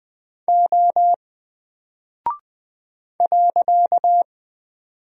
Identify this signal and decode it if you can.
Morse code, heard as a keyed tone, about 20 wpm, 700 Hz. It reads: O .
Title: Single Character (Letter:Number:Punctuation:Prosign) Alt 20wpm